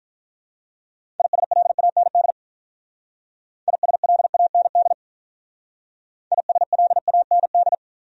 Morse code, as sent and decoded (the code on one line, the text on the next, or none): .. ... .-.. .- -. -.. / .. ... .-.. .- -. -.. / .. ... .-.. .- -. -..
ISLAND ISLAND ISLAND